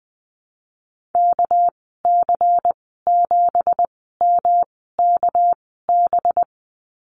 Morse code, heard as a keyed tone, about 20 wpm, spaced as for 14 wpm, 700 Hz. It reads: KC7MKB
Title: 213 Lesson 34g - Number 8 - Call Signs 20wpm